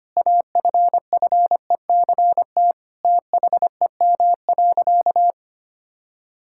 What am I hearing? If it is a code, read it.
Morse code, heard as a keyed tone, about 25 wpm, 700 Hz. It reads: AFFECT THEM.
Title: Sentences from Top 1000 Words - Review Part 6 - 25wpm